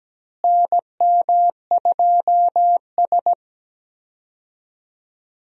Morse code, heard as a keyed tone, about 17 wpm, 700 Hz. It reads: NM2S
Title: USA Call Signs - 4 Characters - Repeated 3x - 17wpm